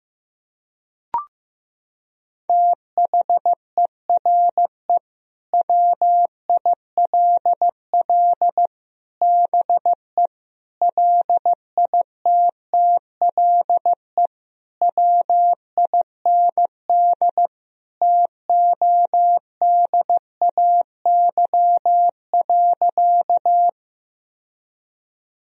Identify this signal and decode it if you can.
Morse code, heard as a keyed tone, about 15 wpm, 700 Hz. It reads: THERE WILL BE LITTLE WIND TODAY.
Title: Sentences from Top 1000 Words - Review Part 2 - 15wpm